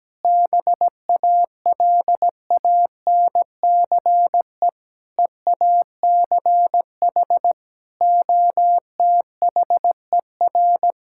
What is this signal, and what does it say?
Morse code, heard as a keyed tone, about 17 wpm, 700 Hz. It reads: BALANCE EACH OTHER
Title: Sentences from Top 1000 Words - Review Part 6 - 17wpm